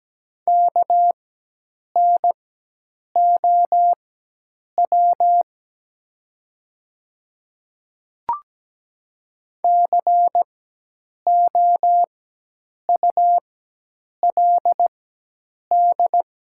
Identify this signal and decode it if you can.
Morse code, heard as a keyed tone, about 17 wpm, spaced as for 8 wpm, 700 Hz. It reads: KNOW COULD
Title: 4x Character Spacing - Top 100 Words - 17wpm